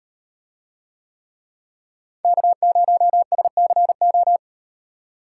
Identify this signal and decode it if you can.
Morse code, heard as a keyed tone, about 38 wpm, 700 Hz. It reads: K0SCO